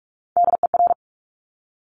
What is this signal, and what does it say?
Morse code, heard as a keyed tone, about 46 wpm, 700 Hz. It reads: DER
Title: Top Three Letter Combinations in Words 45wpm